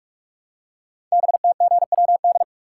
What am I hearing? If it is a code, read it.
Morse code, heard as a keyed tone, about 44 wpm, 700 Hz. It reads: BTGWD
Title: Code Groups - 5 Letters - 45wpm